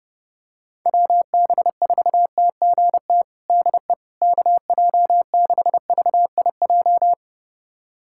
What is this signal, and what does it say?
Morse code, heard as a keyed tone, about 30 wpm, 700 Hz. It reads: WB4TGT DE KJ6VIJ
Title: USA Callsign DE Callsign - 6 Characters - 30wpm